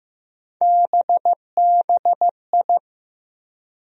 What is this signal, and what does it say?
Morse code, heard as a keyed tone, about 16 wpm, 700 Hz. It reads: BBI
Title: USA Call Sign Postfixes - 3 Letters 15wpm